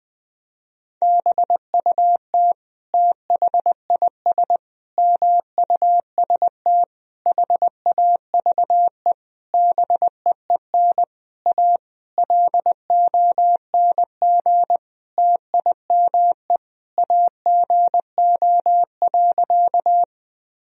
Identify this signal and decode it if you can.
Morse code, heard as a keyed tone, about 21 wpm, 700 Hz. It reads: BUT THIS MUST HAVE BEEN A LONG TIME AGO.
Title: Sentences from Top 1000 Words - Review Part 3 - 20wpm